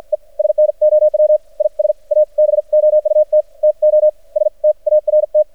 Morse code, heard as a keyed tone, about 56 wpm, spaced as for 44 wpm, 600 Hz. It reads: E SNOW IS ABOUT TO START
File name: archived / CRSnet